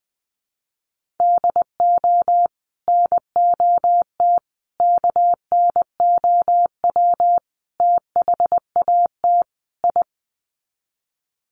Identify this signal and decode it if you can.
Morse code, heard as a keyed tone, about 20 wpm, 700 Hz. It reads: DO NOT KNOW THAT I
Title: Sets of 5 Words Encore2 20wpm